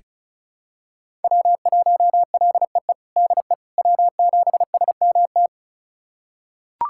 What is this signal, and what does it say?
Morse code, heard as a keyed tone, about 36 wpm, 700 Hz. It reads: W1LEE DE W7SMT
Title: USA Callsign DE Callsign - 5 Characters - 35wpm